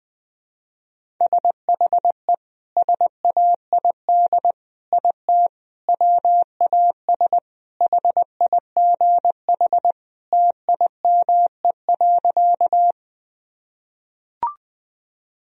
Morse code, heard as a keyed tone, about 19 wpm, 700 Hz. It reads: SHE SAID IT WAS HIGH TIME.
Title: Sentences from Top 1000 Words - Review Part 1 - 20wpm